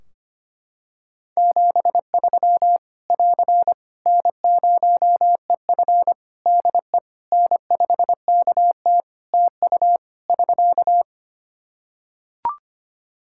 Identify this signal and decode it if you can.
Morse code, heard as a keyed tone, about 25 wpm, 700 Hz. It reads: 73 <AR> N0EF DE N5KT TU <SK>
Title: Practice QSO - Exchange 4 - Ending - 25wpm